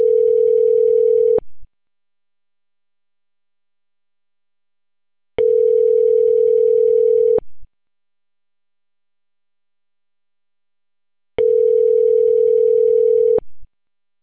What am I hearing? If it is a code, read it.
Ringback tone (North America), 440 + 480 Hz.